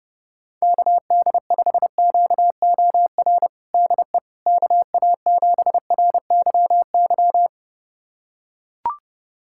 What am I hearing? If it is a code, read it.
Morse code, heard as a keyed tone, about 30 wpm, 700 Hz. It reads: KD5QOR DE KA7RYY
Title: USA Callsign DE Callsign - 6 Characters - 30wpm